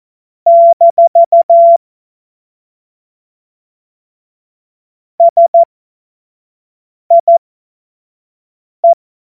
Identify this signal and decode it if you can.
Morse code, heard as a keyed tone, about 14 wpm, spaced as for 5 wpm, 680 Hz. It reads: - SIE
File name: Morsekurs / Witze / Witz 13 Mai 2025 25 F70